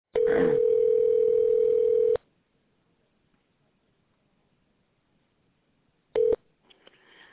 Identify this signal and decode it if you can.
Ringback tone (North America), 440 + 480 Hz.